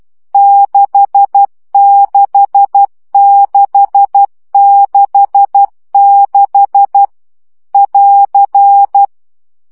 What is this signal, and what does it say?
Morse code, heard as a keyed tone, about 12 wpm, 800 Hz. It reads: 66666 <AR>
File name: CW gyakorlas 6 10 baud